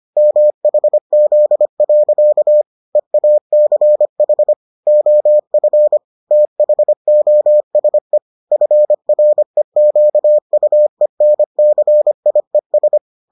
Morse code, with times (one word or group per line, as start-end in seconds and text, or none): 0.2-2.6 MHZ.
3.0-4.5 EACH
4.9-6.0 OF
6.3-8.2 THOSE
8.5-13.0 FREQUENCIES